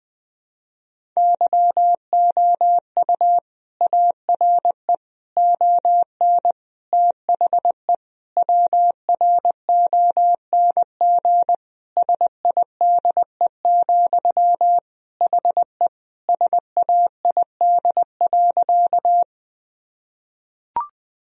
Morse code, heard as a keyed tone, about 20 wpm, 700 Hz. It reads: YOU ARE ON THE WRONG SIDE, HE SAID.
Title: Sentences from Top 1000 Words - Review Part 1 - 20wpm